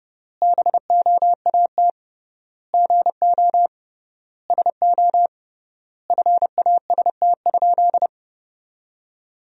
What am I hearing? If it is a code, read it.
Morse code, heard as a keyed tone, about 32 wpm, 700 Hz. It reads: BOAT GO SO FAST?
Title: 3x Word-Spacing - Sentences from Top 500 Words - 30wpm